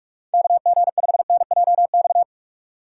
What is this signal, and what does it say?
Morse code, heard as a keyed tone, about 46 wpm, 700 Hz. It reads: KG5NJX